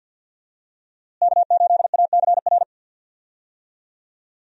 Morse code, heard as a keyed tone, about 51 wpm, 700 Hz. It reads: K8ACR
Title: USA Call Signs - 5 Characters - Repeated 2x - 50wpm